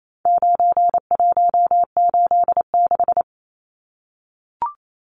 Morse code, heard as a keyed tone, about 28 wpm, 700 Hz. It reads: 9186
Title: Numbers - 4-Digit 28wpm